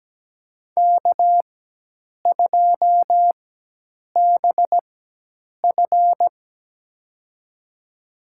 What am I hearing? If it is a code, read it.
Morse code, heard as a keyed tone, about 17 wpm, spaced as for 8 wpm, 700 Hz. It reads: K2BF
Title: USA Call Signs - 4 Characters - 4x Character Spacing - 17wpm